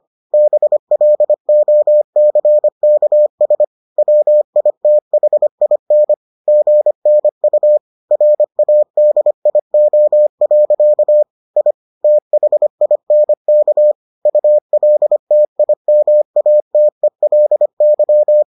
Morse code, heard as a keyed tone, about 25 wpm, 600 Hz. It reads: BLOCKS WITHIN GNU RADIO. I THINK ULTIMATELY